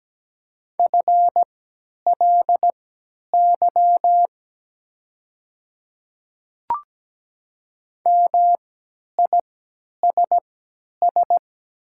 Morse code, heard as a keyed tone, about 17 wpm, spaced as for 10 wpm, 700 Hz. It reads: FLY MISS
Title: 3x Character Spacing - Top 500 Words - 17wpm